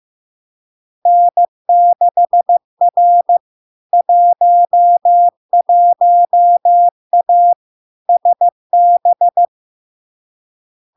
Morse code, heard as a keyed tone, about 15 wpm, 700 Hz. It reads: N6R 11A SB
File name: LARK-FD-2022-Log-in-CW-15wpm-700-Hz